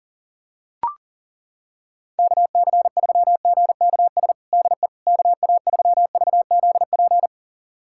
Morse code, heard as a keyed tone, about 40 wpm, 700 Hz. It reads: KC3GKS DE KA3VZP
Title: USA Callsign DE Callsign - 6 Characters - 40wpm